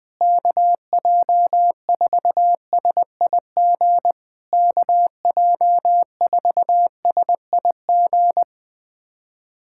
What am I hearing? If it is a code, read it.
Morse code, heard as a keyed tone, about 18 wpm, 700 Hz. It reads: KJ4SIG KJ4SIG
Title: SOTA QSO Practice 20wpm